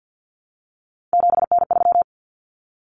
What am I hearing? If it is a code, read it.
Morse code, heard as a keyed tone, about 50 wpm, 700 Hz. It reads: 7N3
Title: Common Cut Numbers - 3-Digit 50wpm